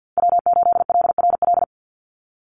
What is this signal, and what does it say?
Morse code, heard as a keyed tone, about 50 wpm, 700 Hz. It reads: W8LRL
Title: USA Call Signs 50wpm